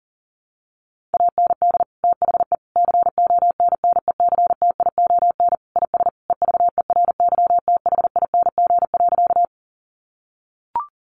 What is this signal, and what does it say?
Morse code, heard as a keyed tone, about 40 wpm, 700 Hz. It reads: AND THE CONNECTION IS EVERYTHING.
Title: Sentences from Top 1300 Words 40wpm